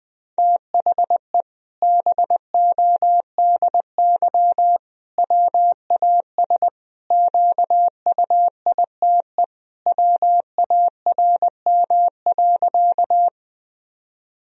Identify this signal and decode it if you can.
Morse code, heard as a keyed tone, about 20 wpm, 700 Hz. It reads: THE BODY WAS QUITE WARM.
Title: Sentences from Top 1000 Words - Review Part 2 - 20wpm